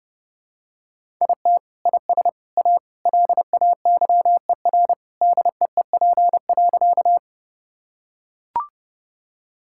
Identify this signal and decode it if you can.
Morse code, heard as a keyed tone, about 30 wpm, 700 Hz. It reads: IT IS A LAYER DEEP.
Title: Sentences from Top 1600 Words 30wpm